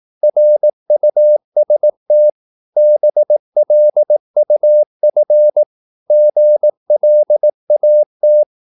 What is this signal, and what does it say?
Morse code, heard as a keyed tone, about 18 wpm, 600 Hz. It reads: RUST BLUF GLAT
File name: Korte ord DK 18wpm